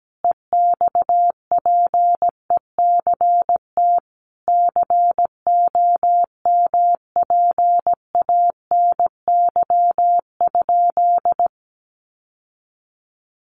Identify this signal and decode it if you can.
Morse code, heard as a keyed tone, about 17 wpm, 700 Hz. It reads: EXPECT COMPANY?
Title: Sentences from Top 1100 Words 17wpm